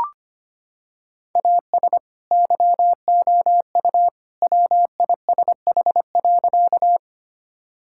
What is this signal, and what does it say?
Morse code, heard as a keyed tone, about 25 wpm, 700 Hz. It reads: AS YOU WISH.
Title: Sentences from Top 1600 Words 25wpm